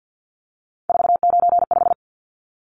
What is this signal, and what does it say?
Morse code, heard as a keyed tone, about 48 wpm, 700 Hz. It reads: 495
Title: Numbers - 3-Digit 50wpm